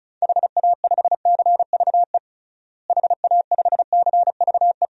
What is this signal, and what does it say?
Morse code, heard as a keyed tone, about 34 wpm, 700 Hz. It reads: HA5CVE HA5CVE
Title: DXCC-100 Call Signs - 6 Characters - Repeated 2x - 35wpm